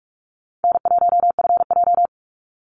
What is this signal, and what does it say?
Morse code, heard as a keyed tone, about 46 wpm, 700 Hz. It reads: N1FJ
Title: NA SOTA Activator Callsigns 45wpm